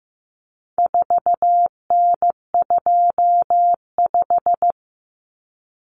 Morse code, heard as a keyed tone, about 15 wpm, 700 Hz. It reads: 4N25
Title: Common Cut Numbers - 4-Digit 15wpm